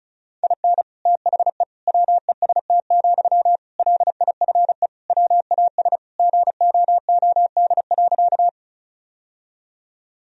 Morse code, heard as a keyed tone, about 34 wpm, 700 Hz. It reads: IN THE WEST, LIFE WAS GOOD.
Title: Sentences from Top 1000 Words - Review Part 2 - 35wpm